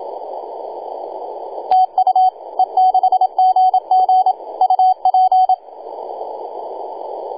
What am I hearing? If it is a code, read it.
Morse code, heard as a keyed tone, about 27 wpm, 750 Hz, with radio noise. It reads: TU E6GG UP
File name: e6gg-18-083-cw